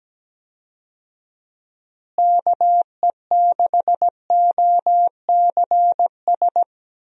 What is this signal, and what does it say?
Morse code, heard as a keyed tone, about 17 wpm, 700 Hz. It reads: KE6OCS
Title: USA Call Signs - 6 Characters - Repeated 3x - 17wpm